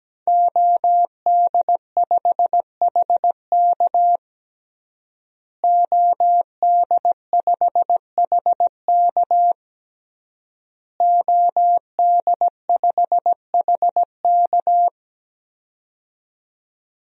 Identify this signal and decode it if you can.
Morse code, heard as a keyed tone, about 17 wpm, 700 Hz. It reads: OD5HK OD5HK OD5HK
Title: DXCC-100 Call Signs - 5 Characters - Repeated 3x - 17wpm